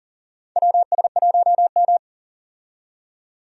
Morse code, heard as a keyed tone, about 40 wpm, 700 Hz. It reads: WS1M
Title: USA Call Signs - 4 Characters - 40wpm